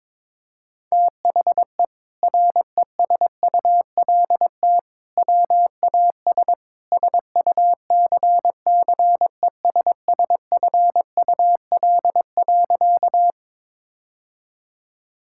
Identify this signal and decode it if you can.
Morse code, heard as a keyed tone, about 22 wpm, 700 Hz. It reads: THE RESULT WAS SUCCESSFUL.
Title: Sentences from Top 1000 Words - Review Part 6 - 22wpm